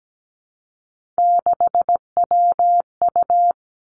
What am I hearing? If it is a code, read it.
Morse code, heard as a keyed tone, about 17 wpm, 700 Hz. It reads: 6WU
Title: Three Challenging Letter-Number Combinations 17wpm